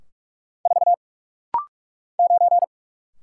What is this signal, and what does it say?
Morse code, heard as a keyed tone, about 44 wpm, 690 Hz. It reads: 4 9
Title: Single Letter-Number - Rapid-Fire 45wpm